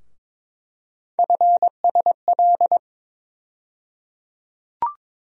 Morse code, heard as a keyed tone, about 22 wpm, 700 Hz. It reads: FSL
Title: USA Call Sign Postfixes - 3 Letters 22wpm